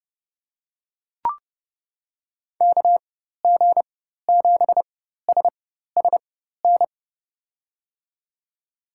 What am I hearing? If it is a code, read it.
Morse code, heard as a keyed tone, about 30 wpm, spaced as for 14 wpm, 700 Hz. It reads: KG7SSN
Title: USA Call Signs - 6 Characters - 4x Character Spacing - 30wpm